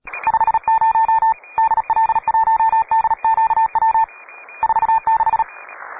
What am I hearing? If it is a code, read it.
Morse code, heard as a keyed tone, about 36 wpm, 890 Hz. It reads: 50 DL1DQW 46
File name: wae / cw pro / rt9w cw